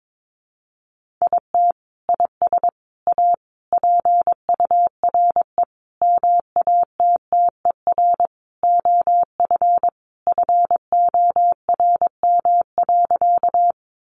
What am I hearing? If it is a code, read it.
Morse code, heard as a keyed tone, about 22 wpm, 700 Hz. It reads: IT IS A PURE MATTER OF FORM.
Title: Sentences from Top 1500 Words 22wpm